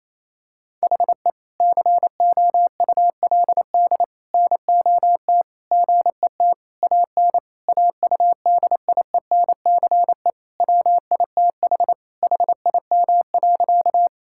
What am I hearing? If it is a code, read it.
Morse code, heard as a keyed tone, about 28 wpm, 700 Hz. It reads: HE COULD NOT GET AN AUDIENCE WITH HIM.
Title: Sentences from Top 1000 Words - Review Part 5 - 28wpm